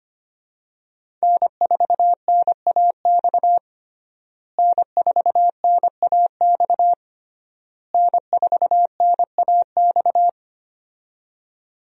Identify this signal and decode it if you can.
Morse code, heard as a keyed tone, about 25 wpm, 700 Hz. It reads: N4NAX N4NAX N4NAX
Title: USA Call Signs - 5 Characters - Repeated 3x - 25wpm